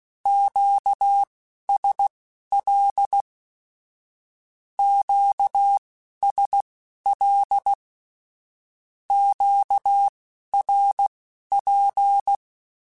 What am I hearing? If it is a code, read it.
Morse code, heard as a keyed tone, about 16 wpm, spaced as for 11 wpm, 800 Hz. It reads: QSL QSL QRP
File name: (spaced) / Q-Calls Spaced-mp3s / Q Calls sp Once @15